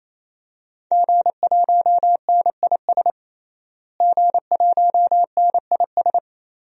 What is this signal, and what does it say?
Morse code, heard as a keyed tone, about 27 wpm, 700 Hz. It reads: G1NIS G1NIS
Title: DXCC-100 Call Signs - 5 Characters - Repeated 3x - 28wpm